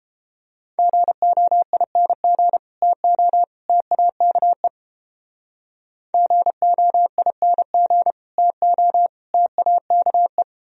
Morse code, heard as a keyed tone, about 31 wpm, 700 Hz. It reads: GOING TO TAKE GOING TO TAKE